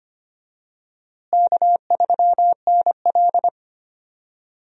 Morse code, heard as a keyed tone, about 25 wpm, 700 Hz. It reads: K3NL
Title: USA Call Signs - 4 Characters - Repeated 3x - 25wpm